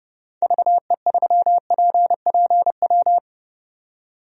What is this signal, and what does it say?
Morse code, heard as a keyed tone, about 30 wpm, 700 Hz. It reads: VE3PPW
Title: DXCC-100 Call Signs - 6 Characters - 30wpm